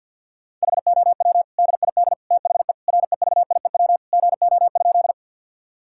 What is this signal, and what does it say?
Morse code, heard as a keyed tone, about 50 wpm, 700 Hz. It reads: HOW DID THE REVIEW GO?
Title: Sentences from Top 1000 Words - Review Part 6 - 50wpm